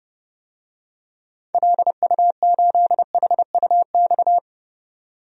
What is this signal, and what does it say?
Morse code, heard as a keyed tone, about 30 wpm, 700 Hz. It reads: LU8HUX
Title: DXCC-100 Call Signs - 6 Characters - 30wpm